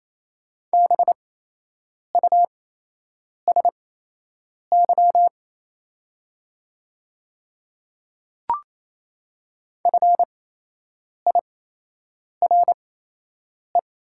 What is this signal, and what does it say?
Morse code, heard as a keyed tone, about 28 wpm, spaced as for 8 wpm, 700 Hz. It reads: BUSY FIRE
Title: 8x Character Spacing - Top 500 Words - 28wpm